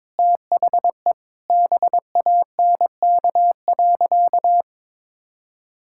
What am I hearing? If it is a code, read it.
Morse code, heard as a keyed tone, about 22 wpm, 700 Hz. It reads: THE BANK.
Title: Sentences from Top 1000 Words - Review Part 6 - 22wpm